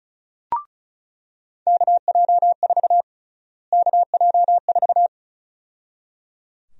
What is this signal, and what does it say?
Morse code, heard as a keyed tone, about 35 wpm, 700 Hz. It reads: KJ4 KJ4
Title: USA Call Sign Prefixes - Repeated 2x - 35wpm